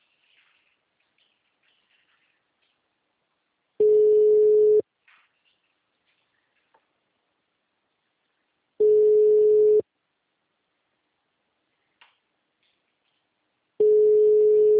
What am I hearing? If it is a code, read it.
Ringback tone (Europe, 425 Hz).
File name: out- -2016- - - .328